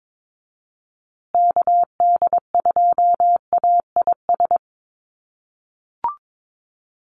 Morse code, heard as a keyed tone, about 22 wpm, 700 Hz. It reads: KD2AIS
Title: USA Call Signs Repeated 2x 22wpm